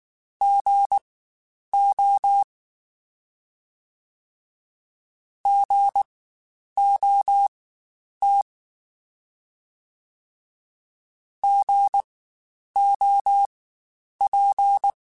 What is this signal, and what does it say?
Morse code, heard as a keyed tone, about 19 wpm, spaced as for 9 wpm, 800 Hz. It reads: GO GOT GOP
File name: Spaced Alphas / Spaced Alphas mp3s / Gs sp @18